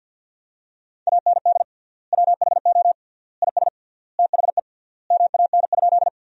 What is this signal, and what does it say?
Morse code, heard as a keyed tone, about 50 wpm, 700 Hz. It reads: AND WHO IS THE MAN?
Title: 3x Word-Spacing - Sentences from Top 200 Words - 50wpm